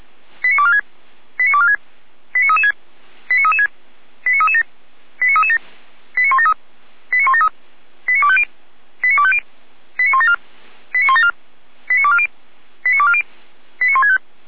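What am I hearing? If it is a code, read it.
Five-tone selective call (ZVEI), heard as quick runs of five tones, tones of about 70 ms. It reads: 89246 89246 89336 89336 89337 89337 89173 89173 89250 89250 89174 89174 89240 89240 89165
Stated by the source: ZVEI five-tone selective call 89160 received on an SDR by Feuerwehr Orlamünde.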